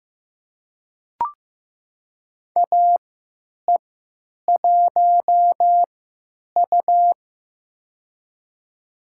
Morse code, heard as a keyed tone, about 15 wpm, spaced as for 9 wpm, 700 Hz. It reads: AE1U
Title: USA Call Signs - 4 Characters - 3x Character Spacing - 15wpm